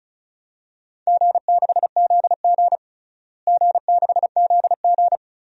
Morse code, heard as a keyed tone, about 36 wpm, 700 Hz. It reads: G6ZG G6ZG
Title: DXCC-100 Call Signs - 4 Characters - Repeated 3x - 35wpm